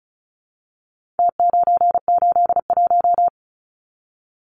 Morse code, heard as a keyed tone, about 34 wpm, 700 Hz. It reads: T981
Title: Numbers - Contest Style - 4-Digit 35wpm